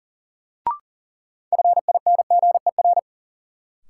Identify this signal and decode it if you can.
Morse code, heard as a keyed tone, about 40 wpm, 700 Hz. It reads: FINGER
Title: Sentences from Top 1600 Words - New Word Drill Alt - 40wpm